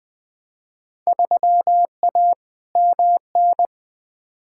Morse code, heard as a keyed tone, about 21 wpm, 700 Hz. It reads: 3A MN
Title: Field Day Exchanges - 20wpm